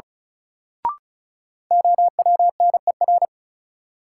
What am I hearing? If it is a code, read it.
Morse code, heard as a keyed tone, about 34 wpm, 700 Hz. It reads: OWNER
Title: Sentences from Top 1600 Words - New Word Drill - 35wpm